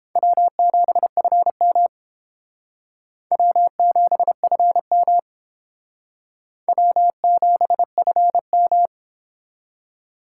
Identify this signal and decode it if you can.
Morse code, heard as a keyed tone, about 31 wpm, 700 Hz. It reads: W7FM W7FM W7FM